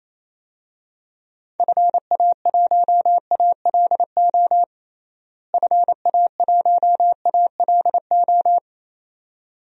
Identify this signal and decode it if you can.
Morse code, heard as a keyed tone, about 28 wpm, 700 Hz. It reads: FA1ALO FA1ALO
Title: DXCC-100 Call Signs - 6 Characters - Repeated 2x - 28wpm